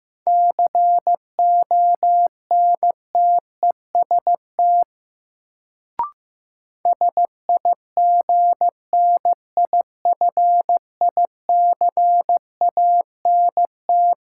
Morse code, heard as a keyed tone, about 16 wpm, 700 Hz. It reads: CONTEST SIGNIFICANT
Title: Sentences from Top 1600 Words - New Word Drill Alt - 15wpm